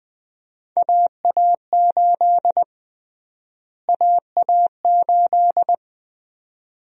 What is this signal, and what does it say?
Morse code, heard as a keyed tone, about 20 wpm, 700 Hz. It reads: AA8 AA8
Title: USA Call Sign Prefixes - Repeated 2x - 20wpm